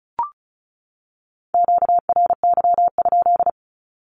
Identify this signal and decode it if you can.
Morse code, heard as a keyed tone, about 34 wpm, 700 Hz. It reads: QRY?
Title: Q-Codes 35wpm